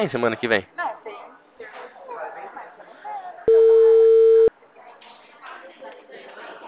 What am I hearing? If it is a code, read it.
Ringback tone (China, 450 Hz), with speech.
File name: out- -2016